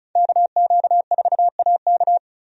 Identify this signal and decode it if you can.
Morse code, heard as a keyed tone, about 34 wpm, 700 Hz. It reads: KQ4AK
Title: USA Call Signs - 5 Characters - Repeated 2x - 35wpm